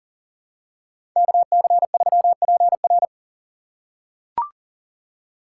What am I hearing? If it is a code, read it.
Morse code, heard as a keyed tone, about 40 wpm, 700 Hz. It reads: KC3PR
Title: USA Call Signs - 5 Characters - 6x Character Spacing - 40wpm